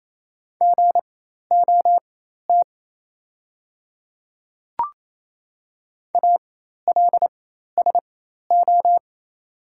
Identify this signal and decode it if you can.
Morse code, heard as a keyed tone, about 28 wpm, spaced as for 13 wpm, 700 Hz. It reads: GOT ALSO
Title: 4x Character Spacing - Top 500 Words - 28wpm